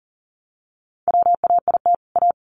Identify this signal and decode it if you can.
Morse code, heard as a keyed tone, about 40 wpm, 700 Hz. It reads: WAIT A
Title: Sentences from Top 1000 Words 40wpm